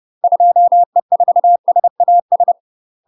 Morse code, heard as a keyed tone, about 30 wpm, 700 Hz. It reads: 2E4SAS
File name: UK MASTER QSOs TRAINING/30WPM